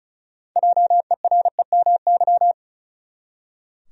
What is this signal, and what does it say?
Morse code, heard as a keyed tone, about 34 wpm, 700 Hz. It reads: JEREMY
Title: USA Common Names - 35wpm